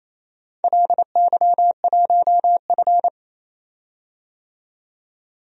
Morse code, heard as a keyed tone, about 27 wpm, 700 Hz. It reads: LY1F
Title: DXCC-100 Call Signs - 4 Characters - 28wpm